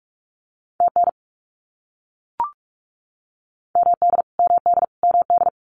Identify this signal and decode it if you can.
Morse code, heard as a keyed tone, about 44 wpm, 700 Hz. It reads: TN MD MD MD
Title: US States Repeated 3x 45wpm